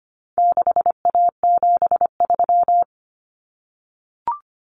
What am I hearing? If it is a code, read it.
Morse code, heard as a keyed tone, about 25 wpm, 700 Hz. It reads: 6A73
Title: Common Cut Numbers - 4-Digit 25wpm